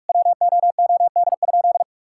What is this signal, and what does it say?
Morse code, heard as a keyed tone, about 44 wpm, 700 Hz. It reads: WOOD?
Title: Sentences from Top 1600 Words 45wpm